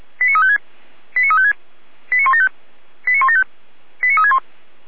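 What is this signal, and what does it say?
Five-tone selective call (ZVEI), heard as quick runs of five tones, tones of about 70 ms. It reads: 89356 89356 89175 89175 89361
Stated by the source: Five-tone ZVEI selective call received by SDR.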